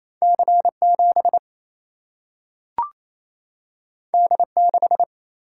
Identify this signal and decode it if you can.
Morse code, heard as a keyed tone, about 28 wpm, 700 Hz. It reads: C7 D6
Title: Single-Letter Single-Number 28wpm